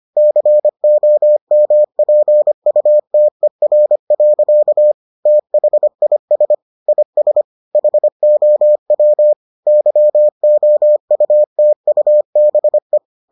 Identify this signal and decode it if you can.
Morse code, heard as a keyed tone, about 25 wpm, 600 Hz. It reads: COMPUTER. THIS IS HOW YOUTUBE